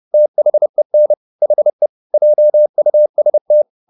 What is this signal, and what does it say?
Morse code, heard as a keyed tone, about 30 wpm, 600 Hz. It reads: THEN HE JUST